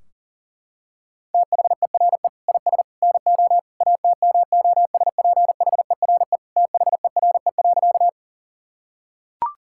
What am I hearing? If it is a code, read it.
Morse code, heard as a keyed tone, about 37 wpm, 700 Hz. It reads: THERE IS NO ATMOSPHERE THERE.
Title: Sentences from Top 1600 Words 40wpm